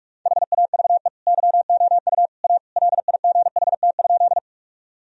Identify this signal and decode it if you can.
Morse code, heard as a keyed tone, about 46 wpm, 700 Hz. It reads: HAVE YOU A LIGHT?
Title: Sentences from Top 1000 Words - Review Part 1 - 45wpm